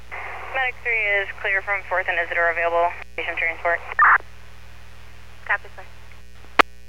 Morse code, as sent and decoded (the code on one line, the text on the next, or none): none